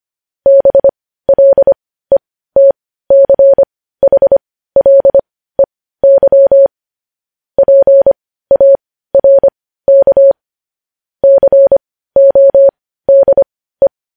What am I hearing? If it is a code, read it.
Morse code, heard as a keyed tone, about 25 wpm, spaced as for 15 wpm, 550 Hz. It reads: BLETCHLEY PARK CODE
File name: podcast / News headlines-1x-25x15